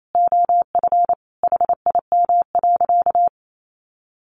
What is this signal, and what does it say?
Morse code, heard as a keyed tone, about 28 wpm, 700 Hz. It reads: OF HIM.
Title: Sentences from Top 1500 Words 28wpm